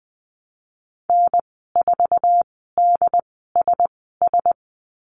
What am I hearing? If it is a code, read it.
Morse code, heard as a keyed tone, about 20 wpm, spaced as for 15 wpm, 700 Hz. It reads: N4DSS
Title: 069 Lesson 12g - Letter d - Call Signs 20wpm